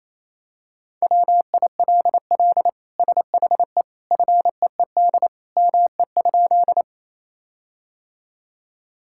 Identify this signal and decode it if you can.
Morse code, heard as a keyed tone, about 28 wpm, 700 Hz. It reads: WILL SHE FEED ME?
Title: Sentences from Top 1600 Words 28wpm